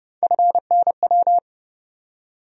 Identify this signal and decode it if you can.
Morse code, heard as a keyed tone, about 30 wpm, 700 Hz. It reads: FNW
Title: USA Call Sign Postfixes - 3 Letters 30wpm